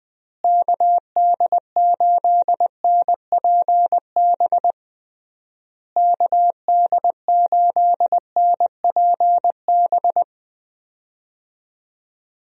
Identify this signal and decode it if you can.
Morse code, heard as a keyed tone, about 21 wpm, 700 Hz. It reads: KD8NPB KD8NPB
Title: USA Call Signs - 6 Characters - Repeated 3x - 20wpm